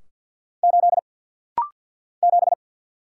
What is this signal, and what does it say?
Morse code, heard as a keyed tone, about 49 wpm, 700 Hz. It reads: 8 7
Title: Single Letter-Number - Rapid-Fire 50wpm